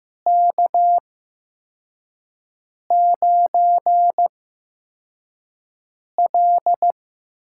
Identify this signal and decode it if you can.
Morse code, heard as a keyed tone, about 15 wpm, spaced as for 4 wpm, 700 Hz. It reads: K9L
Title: USA Call Signs - 8x Character Spacing - 15wpm